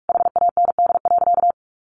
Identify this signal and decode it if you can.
Morse code, heard as a keyed tone, about 44 wpm, 700 Hz. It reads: HAND.
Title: Sentences from Top 1400 Words 45wpm